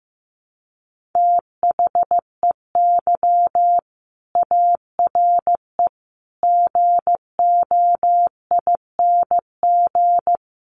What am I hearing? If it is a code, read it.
Morse code, heard as a keyed tone, about 15 wpm, 700 Hz. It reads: THEY ARE GOING
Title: Sets of 4 Words 15wpm